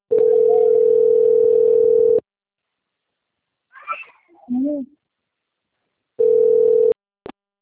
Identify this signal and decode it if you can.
Ringback tone (North America), 440 + 480 Hz.
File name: out- - - - - .15056